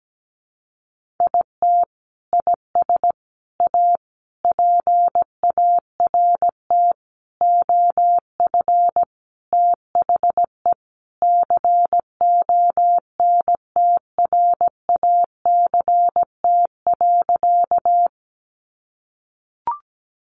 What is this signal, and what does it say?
Morse code, heard as a keyed tone, about 17 wpm, 700 Hz. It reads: IT IS A PART OF THE CONTRACT.
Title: Sentences from Top 1400 Words 17wpm